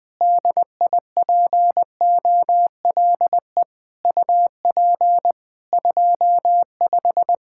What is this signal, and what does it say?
Morse code, heard as a keyed tone, about 20 wpm, 700 Hz. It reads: DIPOLE UP 25
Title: Practice QSO - Exchange 2 - Info - 20wpm